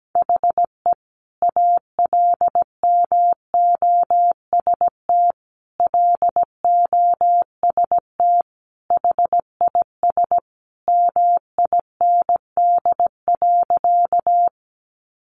Morse code, heard as a keyed tone, about 17 wpm, 700 Hz. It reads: HE ALMOST LOST HIS MIND.
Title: Sentences from Top 1400 Words 17wpm